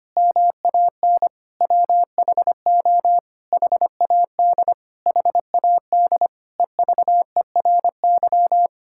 Morse code, heard as a keyed tone, about 25 wpm, 700 Hz. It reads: MAN WHO HAD HAD EVERY
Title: Sentences from Top 1000 Words - Review Part 6 - 25wpm